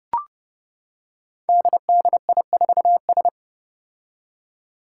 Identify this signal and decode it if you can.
Morse code, heard as a keyed tone, about 32 wpm, 700 Hz. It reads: DDI4S
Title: Five Challenging Letter-Number Combinations 30wpm